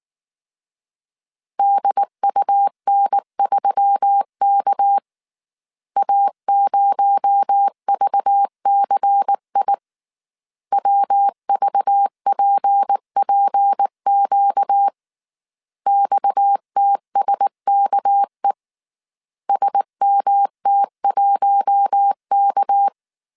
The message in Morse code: DUN3K A0VCI WVPPQ XTSKE SMT1K

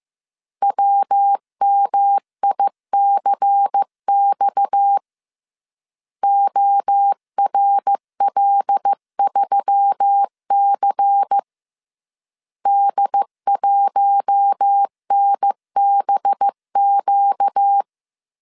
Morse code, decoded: WMICX ORL3C D1NBQ

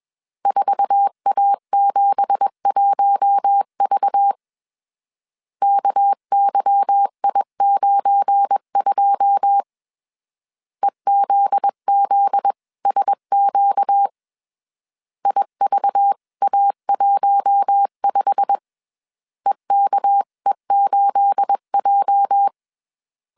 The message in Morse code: 4A71V KYI92 EZZSQ IVA15 EKE8J

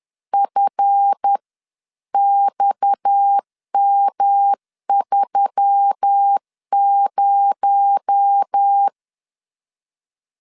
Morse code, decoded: FXM30

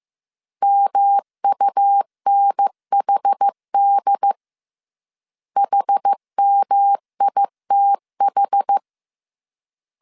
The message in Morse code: MUNHD HMITH

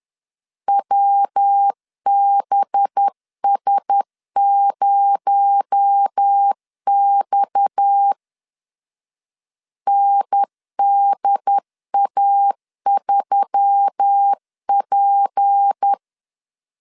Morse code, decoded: WBS0X NDA3P

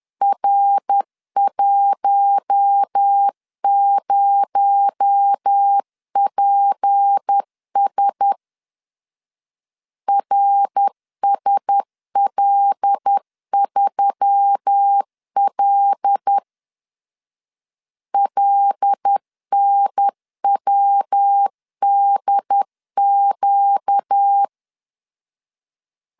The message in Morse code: R10PS RSL3L LNWDQ

